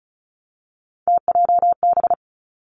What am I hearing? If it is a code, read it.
Morse code, heard as a keyed tone, about 36 wpm, 700 Hz. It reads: TJB
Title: Three Challenging Letter-Number Combinations 35wpm